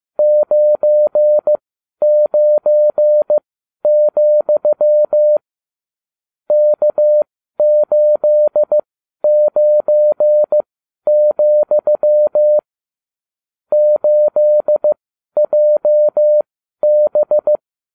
Morse code, heard as a keyed tone, about 15 wpm, spaced as for 11 wpm, 600 Hz. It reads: 99, K89, 8JB